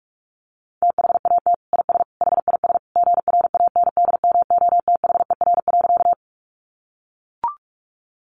Morse code, heard as a keyed tone, about 44 wpm, 700 Hz. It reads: THAT IS HIS GRANDMOTHER.
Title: Sentences from Top 1200 Words 45wpm